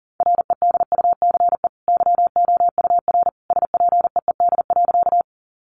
Morse code, heard as a keyed tone, about 40 wpm, 700 Hz. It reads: REDUCE YOUR SPEED.
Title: Sentences from Top 1100 Words 40wpm